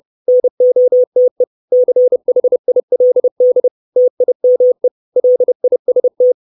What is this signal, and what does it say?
Morse code, heard as a keyed tone, about 30 wpm, 500 Hz. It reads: NOTE CHILD TIME LIST
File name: 632 Words 30WPM